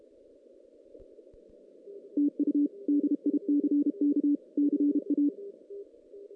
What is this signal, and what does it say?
Morse code, heard as a keyed tone, about 32 wpm, 290 Hz. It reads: TU DICK CA
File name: 2016 JAN CW NAQP / 2016 Jan naqp k5ot - 0-6